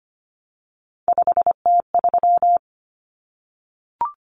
Morse code, heard as a keyed tone, about 25 wpm, 700 Hz. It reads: 5T3
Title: Common Cut Numbers - 3-Digit 25wpm